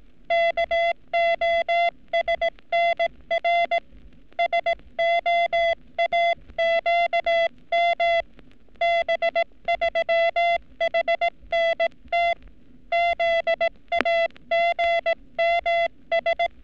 Morse code, heard as a keyed tone, about 18 wpm, 680 Hz. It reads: KOSNR SOAQM B3HNT ZAGMS
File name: training / NAVPERS 11224-11235 / NAVPERS 11233 RA RB